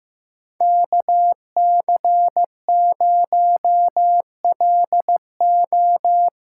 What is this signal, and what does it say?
Morse code, heard as a keyed tone, about 15 wpm, 700 Hz. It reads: KC0LO
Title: USA Call Signs - 6 Characters - 6x Character Spacing - 15wpm